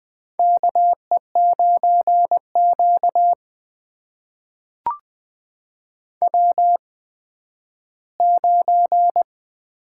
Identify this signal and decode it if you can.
Morse code, heard as a keyed tone, about 20 wpm, spaced as for 12 wpm, 700 Hz. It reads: KE9Q W9
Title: USA Call Signs - 4 Characters - 8x Character Spacing - 20wpm